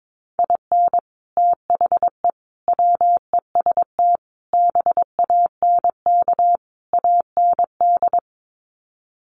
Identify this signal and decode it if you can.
Morse code, heard as a keyed tone, about 22 wpm, 700 Hz. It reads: IN THE WEST BANK AND
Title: Sets of 5 Words Encore2 22wpm